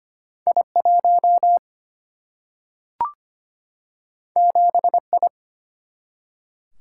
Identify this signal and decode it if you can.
Morse code, heard as a keyed tone, about 25 wpm, 700 Hz. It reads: I1 7I
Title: Single-Letter Single-Number 25wpm